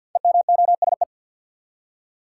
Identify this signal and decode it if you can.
Morse code, heard as a keyed tone, about 49 wpm, 700 Hz. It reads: EMOSE
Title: Code Groups - 5 Letters - 50wpm